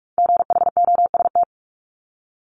Morse code, heard as a keyed tone, about 46 wpm, 690 Hz. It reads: GHOST
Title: Multi-letter Phonemes Repeated 3x 45wpm